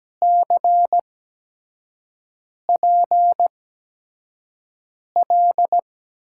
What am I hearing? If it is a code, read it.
Morse code, heard as a keyed tone, about 17 wpm, spaced as for 5 wpm, 700 Hz. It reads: CPL
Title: USA Call Signs - 8x Character Spacing - 17wpm